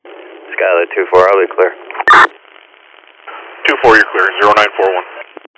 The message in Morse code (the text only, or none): none